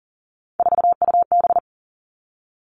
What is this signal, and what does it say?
Morse code, heard as a keyed tone, about 39 wpm, 700 Hz. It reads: 4UB
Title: Three Challenging Letter-Number Combinations 40wpm